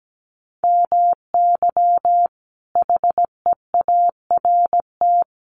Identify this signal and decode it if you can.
Morse code, heard as a keyed tone, about 17 wpm, 700 Hz. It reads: MY HEART
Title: Sentences from Top 1200 Words 17wpm